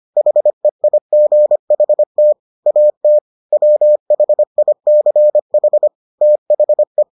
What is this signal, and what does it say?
Morse code, heard as a keyed tone, about 25 wpm, 600 Hz. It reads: HEIGHT AT WHICH THE